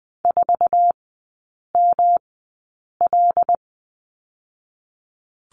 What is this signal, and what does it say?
Morse code, heard as a keyed tone, about 20 wpm, spaced as for 8 wpm, 700 Hz. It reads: 4ML
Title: 127 Lesson 21e - Number 6 - Three Characters 20wpm